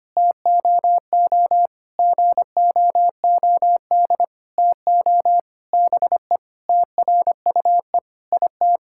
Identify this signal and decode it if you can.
Morse code, heard as a keyed tone, about 25 wpm, 700 Hz. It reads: TOO GOOD TO BE TRUE IT
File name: practice / 25wpm / Sentences with 9-Word Phrases - 25wpm